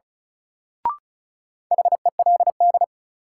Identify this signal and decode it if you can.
Morse code, heard as a keyed tone, about 36 wpm, 700 Hz. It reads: HELD
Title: Sentences from Top 1600 Words - New Word Drill Alt - 35wpm